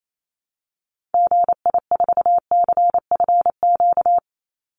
Morse code, heard as a keyed tone, about 28 wpm, 700 Hz. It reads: GI4CFQ
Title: Call Signs Repeated 3x - 28wpm